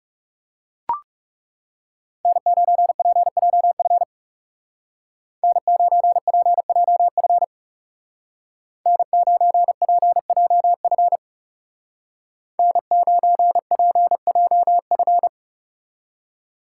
Morse code, heard as a keyed tone, about 39 wpm, 700 Hz. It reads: N9PJF N9PJF N9PJF N9PJF